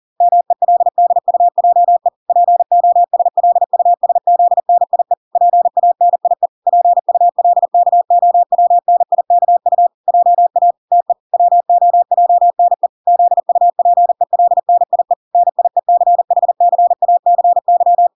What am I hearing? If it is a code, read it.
Morse code, heard as a keyed tone, about 40 wpm, 700 Hz. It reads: MELDUJE POSLUSZNIE PANIE PULKOWNIKU JA TE WOJNE ZUPELNIE NIECHCACY